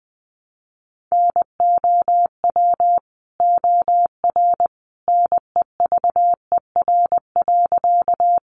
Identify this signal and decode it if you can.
Morse code, heard as a keyed tone, about 20 wpm, 700 Hz. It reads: NOW OR NEVER.
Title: Sentences from Top 300 Words 20wpm